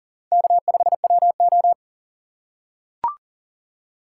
Morse code, heard as a keyed tone, about 40 wpm, 700 Hz. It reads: K5WO
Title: USA Call Signs - 2x Character Spacing - 40wpm